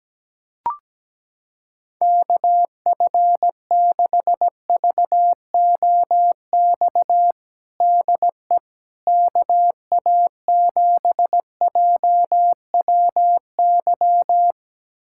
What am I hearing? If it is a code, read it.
Morse code, heard as a keyed tone, about 17 wpm, 700 Hz. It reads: KF6VOX DE KA7JWY
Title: USA Callsign DE Callsign - 6 Characters - 17wpm